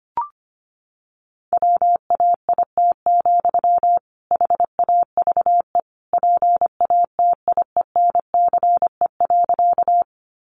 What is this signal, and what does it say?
Morse code, heard as a keyed tone, about 25 wpm, 700 Hz. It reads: WAIT, HAVE PATIENCE.
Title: Sentences from Top 1300 Words 25wpm